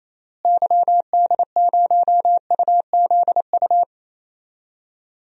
Morse code, heard as a keyed tone, about 28 wpm, 700 Hz. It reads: YD0UZU
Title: DXCC-100 Call Signs - 6 Characters - Repeated 3x - 28wpm